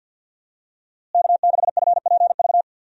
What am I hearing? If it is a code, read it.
Morse code, heard as a keyed tone, about 52 wpm, 700 Hz. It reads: K6FPV
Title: USA Call Signs - 5 Characters - Repeated 3x - 50wpm